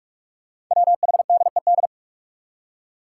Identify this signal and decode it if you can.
Morse code, heard as a keyed tone, about 46 wpm, 700 Hz. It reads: WHDED
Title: Five Challenging Letter-Number Combinations 45wpm